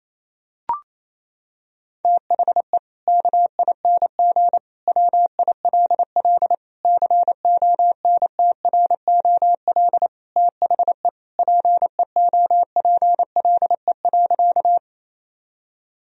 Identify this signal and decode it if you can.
Morse code, heard as a keyed tone, about 28 wpm, 700 Hz. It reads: THE KING WILL CONTROL THE PEOPLE.
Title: Sentences from Top 1000 Words - Review Part 3 - 28wpm